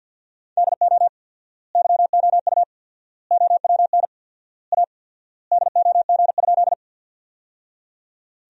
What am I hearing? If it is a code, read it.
Morse code, heard as a keyed tone, about 50 wpm, 700 Hz. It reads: DO YOU OWN A DOG?
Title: 4x Word-Spacing - Sentences from Top 400 Words - 50wpm